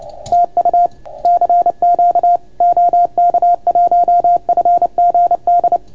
Morse code, heard as a keyed tone, about 30 wpm, 690 Hz. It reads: TU CQ OK1FGD